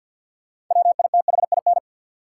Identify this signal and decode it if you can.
Morse code, heard as a keyed tone, about 53 wpm, 700 Hz. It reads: WITHIN
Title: Sets of 2 Words - New Word Drill - 50wpm